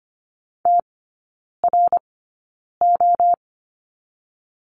Morse code, heard as a keyed tone, about 25 wpm, spaced as for 9 wpm, 690 Hz. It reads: TRO
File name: Lesson 14e - Number 2 - Three Characters 25wpm